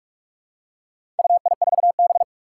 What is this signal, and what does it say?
Morse code, heard as a keyed tone, about 41 wpm, 700 Hz. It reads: UI4B
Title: Four Challenging Letter-Number Combinations 45wpm